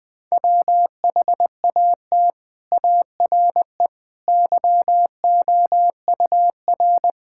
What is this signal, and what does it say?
Morse code, heard as a keyed tone, about 20 wpm, 700 Hz. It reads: WHAT ARE YOUR
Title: Sentences with 6-Word Phrases - Spoken Beginning - 20wpm